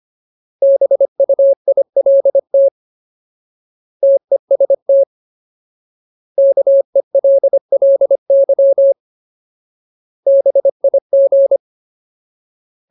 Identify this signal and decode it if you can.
Morse code, heard as a keyed tone, about 25 wpm, 550 Hz. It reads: BUILT TEST KELLY BIG
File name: Ultimate Word List / 25WPM Ultimate